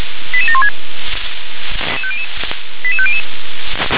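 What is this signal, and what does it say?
Five-tone selective call (ZVEI), heard as quick runs of five tones, tones of about 70 ms. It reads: 88716 88500
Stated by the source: ZVEI five-tone call 88761 received by SDR.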